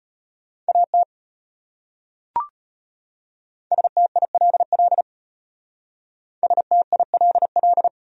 Morse code, heard as a keyed tone, about 36 wpm, 700 Hz. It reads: AT STILL STILL